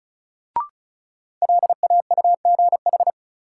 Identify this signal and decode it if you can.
Morse code, heard as a keyed tone, about 36 wpm, 700 Hz. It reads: LAUGH
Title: Sentences from Top 500 Words - New Word Drill - 35wpm